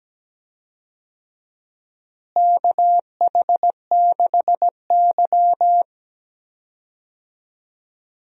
Morse code, heard as a keyed tone, about 17 wpm, 700 Hz. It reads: KH6Y
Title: DXCC-100 Call Signs - 4 Characters - Repeated 2x - 17wpm